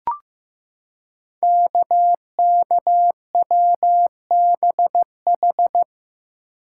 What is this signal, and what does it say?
Morse code, heard as a keyed tone, about 15 wpm, 700 Hz. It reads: KKWBH
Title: Code Groups - 5 Letters - 15wpm